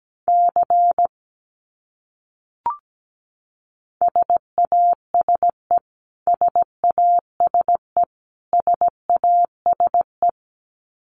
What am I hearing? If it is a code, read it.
Morse code, heard as a keyed tone, about 17 wpm, 700 Hz. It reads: C SASE SASE SASE
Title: Abbreviations Repeated 3x - 17wpm